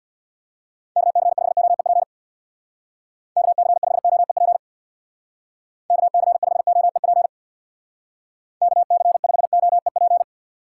Morse code, heard as a keyed tone, about 61 wpm, 700 Hz. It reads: KC5OEP KC5OEP KC5OEP KC5OEP
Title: Speed Racing - USA 2x3 Call Signs 50wpm